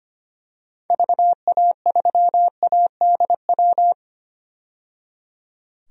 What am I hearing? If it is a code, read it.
Morse code, heard as a keyed tone, about 25 wpm, 700 Hz. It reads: VA3ADW
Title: DXCC-100 Call Signs - 6 Characters - 25wpm